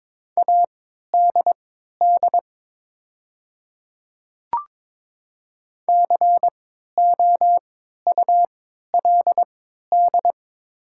Morse code, heard as a keyed tone, about 22 wpm, spaced as for 13 wpm, 700 Hz. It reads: ADD COULD
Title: 3x Character Spacing - Top 200 Words - 22wpm